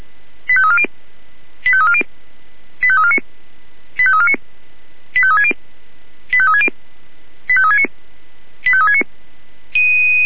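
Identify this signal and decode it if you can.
Five-tone selective call (ZVEI), heard as quick runs of five tones, tones of about 70 ms. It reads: 85340 85340 85349 85349 85360 85360 85369 85369
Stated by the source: ZVEI five-tone call 85349 received on SDR.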